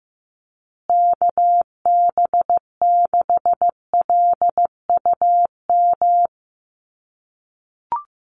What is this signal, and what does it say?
Morse code, heard as a keyed tone, about 15 wpm, 700 Hz. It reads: KB6LUM
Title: USA Call Signs Repeated 2x 15wpm